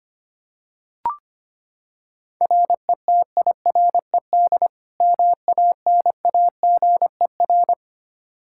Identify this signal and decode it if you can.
Morse code, heard as a keyed tone, about 25 wpm, 700 Hz. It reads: RETIRED MANAGER
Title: K1USN SST QSO Practice 25wpm